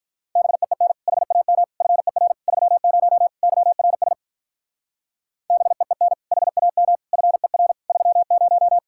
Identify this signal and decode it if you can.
Morse code, heard as a keyed tone, about 43 wpm, 700 Hz. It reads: BEEN HAM FER 30 YRS BEEN HAM FER 30
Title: Speed-Racing Common QSO Phrases - 35wpm